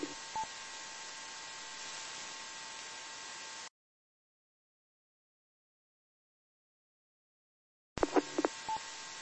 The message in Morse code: none

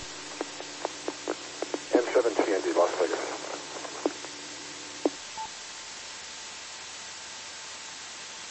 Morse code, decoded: none

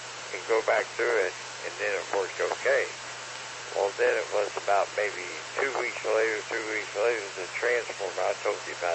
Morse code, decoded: none

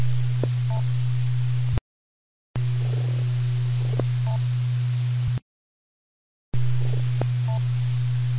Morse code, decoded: none

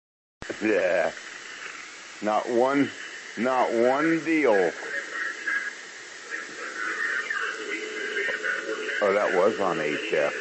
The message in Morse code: none

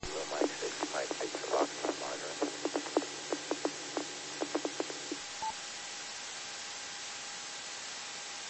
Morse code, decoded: none

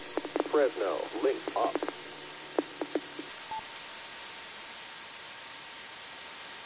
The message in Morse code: none

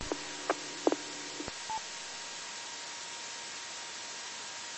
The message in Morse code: none

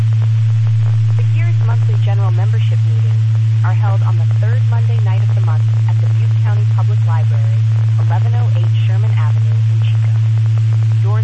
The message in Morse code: none

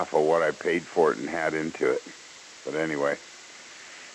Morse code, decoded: none

